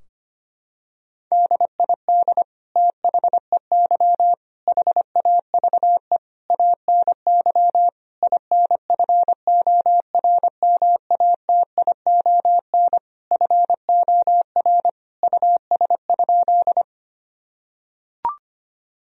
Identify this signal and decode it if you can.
Morse code, heard as a keyed tone, about 25 wpm, 700 Hz. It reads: DID THEY HAVE ANY INFORMATION FOR US?
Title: Sentences from Top 1000 Words - Review Part 3 - 25wpm